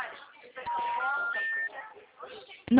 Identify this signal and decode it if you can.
Special information tone, with speech.